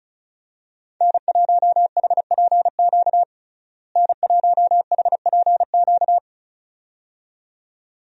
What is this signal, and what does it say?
Morse code, heard as a keyed tone, about 34 wpm, 700 Hz. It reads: N1HPQ N1HPQ
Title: USA Call Signs - 5 Characters - Repeated 2x - 35wpm